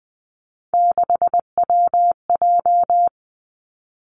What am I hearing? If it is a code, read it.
Morse code, heard as a keyed tone, about 20 wpm, 700 Hz. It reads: 6WJ
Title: Three Challenging Letter-Number Combinations 20wpm